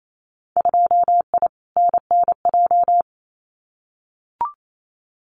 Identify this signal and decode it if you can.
Morse code, heard as a keyed tone, about 28 wpm, 700 Hz. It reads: 2I NNJ
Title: Winter Field Day Exchanges Encore 28wpm